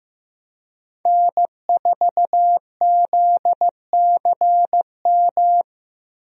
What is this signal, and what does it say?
Morse code, heard as a keyed tone, about 15 wpm, 700 Hz. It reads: N4ZCM
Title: USA Call Signs - 5 Characters - 15wpm